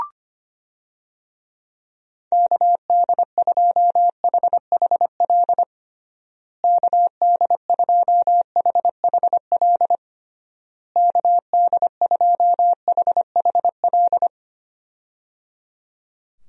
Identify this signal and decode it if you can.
Morse code, heard as a keyed tone, about 25 wpm, 700 Hz. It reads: KD2HHL KD2HHL KD2HHL
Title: USA Call Signs - 6 Characters - Repeated 3x - 25wpm